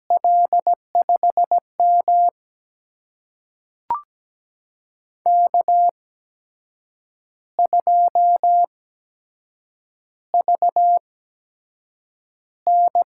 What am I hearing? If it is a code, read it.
Morse code, heard as a keyed tone, about 17 wpm, spaced as for 7 wpm, 700 Hz. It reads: L5M K2VN
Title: USA Call Signs - 4 Characters - 8x Character Spacing - 17wpm